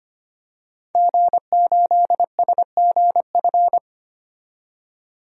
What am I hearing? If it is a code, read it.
Morse code, heard as a keyed tone, about 25 wpm, 700 Hz. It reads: G8SGF
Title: DXCC-100 Call Signs - 5 Characters - 25wpm